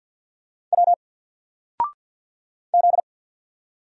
Morse code, heard as a keyed tone, about 50 wpm, 700 Hz. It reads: W Z
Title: Single Letter-Number 50wpm